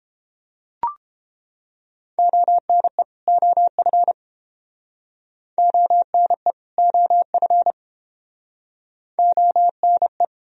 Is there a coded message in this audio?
Yes: Morse code from 2.2 to 10.2 s, heard as a keyed tone, about 30 wpm, 700 Hz. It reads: ONE OF ONE OF ONE